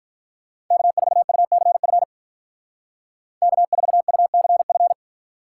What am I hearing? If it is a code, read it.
Morse code, heard as a keyed tone, about 45 wpm, 700 Hz. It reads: K4UCF K4UCF